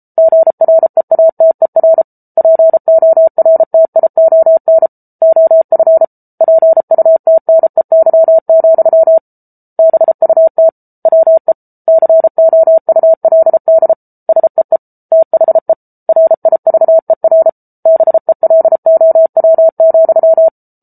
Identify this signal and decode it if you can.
Morse code, heard as a keyed tone, about 34 wpm, 650 Hz. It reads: GREATER PORTION OF PUTNEY, BUT WE COULD SEE THE RIVER BELOW,